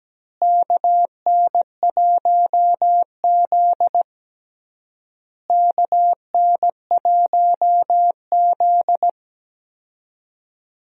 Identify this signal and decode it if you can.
Morse code, heard as a keyed tone, about 17 wpm, 700 Hz. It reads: KN1Z KN1Z
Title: USA Call Signs - 4 Characters - Repeated 3x - 17wpm